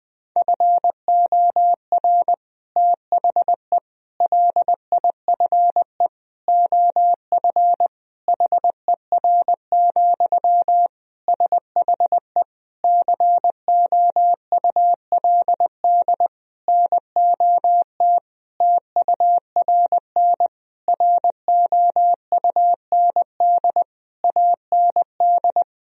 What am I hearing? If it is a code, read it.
Morse code, heard as a keyed tone, about 20 wpm, 700 Hz. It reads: FOR THE LIFE OF HER, SHE COULD NOT TURN ROUND AND
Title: Sentences from Top 1000 Words - Review Part 2 - 20wpm